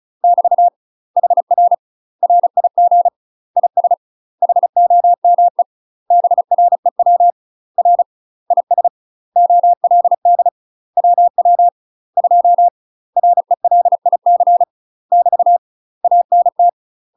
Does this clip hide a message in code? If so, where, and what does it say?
Yes: Morse code from 0.2 to 16.7 s, heard as a keyed tone, about 36 wpm, 700 Hz. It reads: <BT> HR RIG IS HOME BREW R IS OLD WW 2 RELIC <BT> ANT